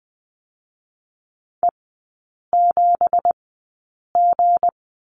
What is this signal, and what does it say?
Morse code, heard as a keyed tone, about 20 wpm, spaced as for 8 wpm, 700 Hz. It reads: E7G
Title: 166 Lesson 27e - Number 7 - Three Characters 20wpm